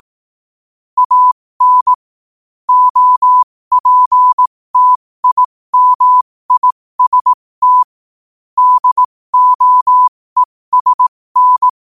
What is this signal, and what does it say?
Morse code, heard as a keyed tone, about 18 wpm, spaced as for 15 wpm, 1000 Hz. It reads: AN OPTIMIST DOESN
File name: qotdfiles / 15wpm